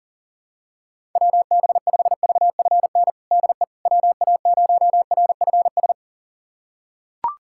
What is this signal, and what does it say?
Morse code, heard as a keyed tone, about 40 wpm, 700 Hz. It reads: WB5VFN DE WA0RFS
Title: USA Callsign DE Callsign - 6 Characters - 40wpm